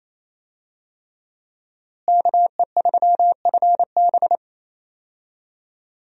Morse code, heard as a keyed tone, about 28 wpm, 700 Hz. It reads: KE3FB
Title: USA Call Signs - 5 Characters - Repeated 3x - 28wpm